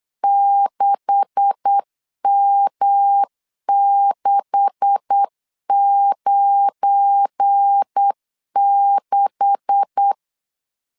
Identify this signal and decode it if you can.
Morse code, heard as a keyed tone, about 8 wpm, 800 Hz. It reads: 6M696
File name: LEZIONI / 9 286 8wpm 3425